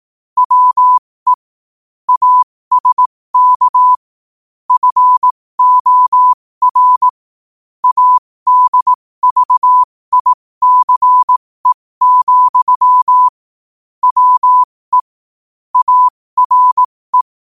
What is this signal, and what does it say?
Morse code, heard as a keyed tone, about 18 wpm, spaced as for 15 wpm, 1000 Hz. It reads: WE ASK FOR ADVICE, WE ARE
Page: New quotes every day in morse code at 15 Words per minute.